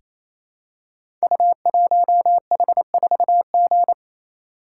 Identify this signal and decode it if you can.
Morse code, heard as a keyed tone, about 29 wpm, 700 Hz. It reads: U1H4G
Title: Code Groups - 5 Letter:Numbers - 28wpm